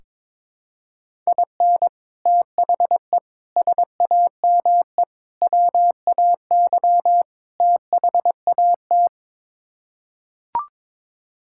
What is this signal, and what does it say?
Morse code, heard as a keyed tone, about 22 wpm, 700 Hz. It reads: IN THE SAME WAY THAT
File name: practice / 22wpm / Build A Phrase - 5-Word Phrases - 22wpm